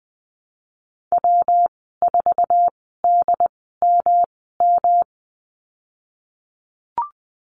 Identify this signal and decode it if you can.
Morse code, heard as a keyed tone, about 20 wpm, spaced as for 15 wpm, 700 Hz. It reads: W4DMM
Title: 122 Lesson 20g - Number 3 - Call Signs 20wpm